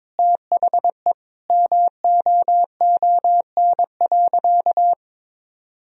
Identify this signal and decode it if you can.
Morse code, heard as a keyed tone, about 22 wpm, 700 Hz. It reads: THE MOON.
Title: Sentences from Top 1000 Words - Review Part 3 - 22wpm